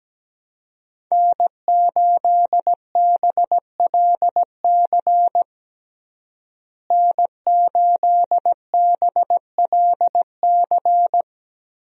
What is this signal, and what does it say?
Morse code, heard as a keyed tone, about 17 wpm, 700 Hz. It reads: N8BLC N8BLC
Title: USA Call Signs - 5 Characters - Repeated 2x - 17wpm